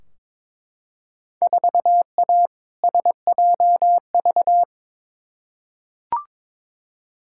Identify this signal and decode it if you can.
Morse code, heard as a keyed tone, about 22 wpm, 700 Hz. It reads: 4A SJV
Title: Field Day Exchanges Encore - 22wpm